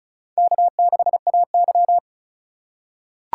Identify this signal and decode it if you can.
Morse code, heard as a keyed tone, about 36 wpm, 700 Hz. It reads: K6AY
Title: USA Call Signs - 4 Characters - 8x Character Spacing - 35wpm